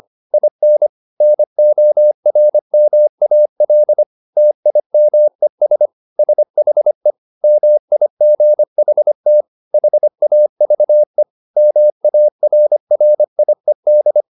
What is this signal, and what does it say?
Morse code, heard as a keyed tone, about 25 wpm, 600 Hz. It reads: IN NORMAL TIMES SHE MIGHT HAVE MARRIED